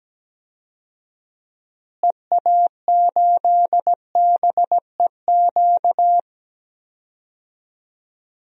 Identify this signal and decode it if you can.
Morse code, heard as a keyed tone, about 17 wpm, 700 Hz. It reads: EA8BEQ
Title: DXCC-100 Call Signs - 6 Characters - Repeated 2x - 17wpm